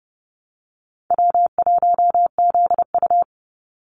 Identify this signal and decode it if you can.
Morse code, heard as a keyed tone, about 30 wpm, 700 Hz. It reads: W1ZU
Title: NA SOTA Activator Callsigns 30wpm